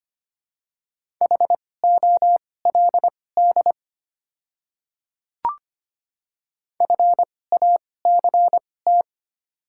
Morse code, heard as a keyed tone, about 25 wpm, spaced as for 18 wpm, 700 Hz. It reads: HOLD FACT
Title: 2x Character Spacing - Top 500 Words - 25wpm